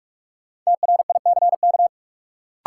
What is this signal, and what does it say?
Morse code, heard as a keyed tone, about 46 wpm, 700 Hz. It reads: TRICK
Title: Sentences from Top 1500 Words - New Word Drill Alt - 45wpm